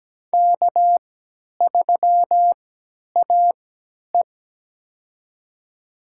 Morse code, heard as a keyed tone, about 17 wpm, spaced as for 10 wpm, 700 Hz. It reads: K3AE
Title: USA Call Signs - 4 Characters - 3x Character Spacing - 17wpm